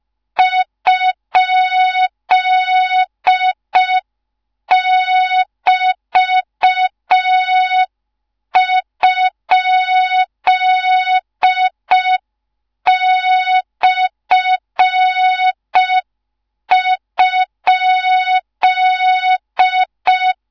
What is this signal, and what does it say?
Morse code, heard as a keyed tone, about 5 wpm, 3700 Hz. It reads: ?<BT>?/?
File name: Morze-HA2MN / hang / 13IRASJELEK1 HA2MN